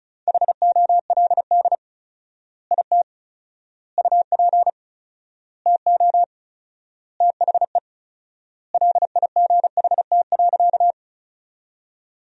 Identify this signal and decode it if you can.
Morse code, heard as a keyed tone, about 36 wpm, 700 Hz. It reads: HOLD IT UP TO THE LIGHT.
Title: 4x Word-Spacing - Sentences from Top 400 Words - 35wpm